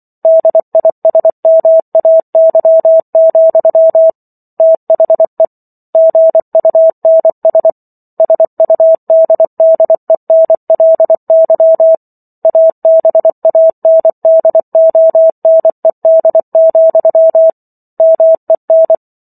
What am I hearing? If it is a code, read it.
Morse code, heard as a keyed tone, about 24 wpm, 650 Hz. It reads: DISMAY, THE GUNS SUDDENLY ABANDONED, MEN